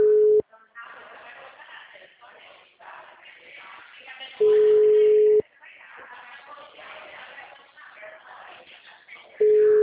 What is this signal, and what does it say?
Ringback tone (Europe, 425 Hz), with speech.
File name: out- -2030- - - .18143